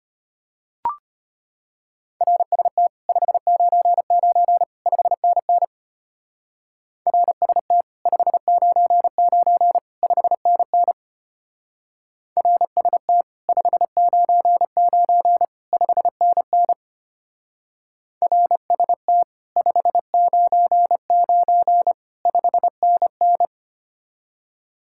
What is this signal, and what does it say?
Morse code, heard as a keyed tone, about 31 wpm, 700 Hz. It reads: RST 599 5NN RST 599 5NN RST 599 5NN RST 599 5NN
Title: Speed-Racing Common QSO Phrases - 25wpm